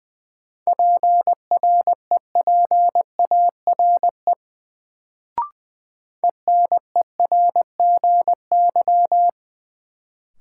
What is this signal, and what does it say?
Morse code, heard as a keyed tone, about 20 wpm, 700 Hz. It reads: PREPARE ENERGY
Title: Sentences from Top 900 Words - New Word Drill - 20wpm